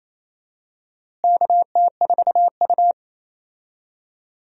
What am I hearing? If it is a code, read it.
Morse code, heard as a keyed tone, about 28 wpm, 700 Hz. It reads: KT4U
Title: USA Call Signs - 4 Characters - Repeated 3x - 28wpm